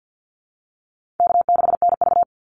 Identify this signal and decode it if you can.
Morse code, heard as a keyed tone, about 50 wpm, 700 Hz. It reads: K6NV
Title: Easy Call Signs Repeated 2x - 50wpm